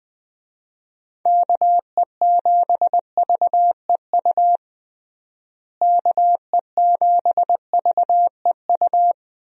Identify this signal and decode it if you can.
Morse code, heard as a keyed tone, about 21 wpm, 700 Hz. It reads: KE7VEU KE7VEU
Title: USA Call Signs - 6 Characters - Repeated 2x - 20wpm